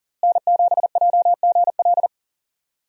Morse code, heard as a keyed tone, about 40 wpm, 700 Hz. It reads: N7JGL
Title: USA Call Signs - 5 Characters - Repeated 3x - 40wpm